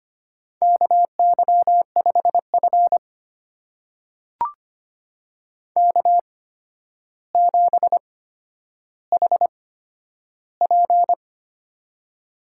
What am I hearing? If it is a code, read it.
Morse code, heard as a keyed tone, about 25 wpm, spaced as for 11 wpm, 700 Hz. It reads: KY5F K7HP
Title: USA Call Signs - 4 Characters - 8x Character Spacing - 25wpm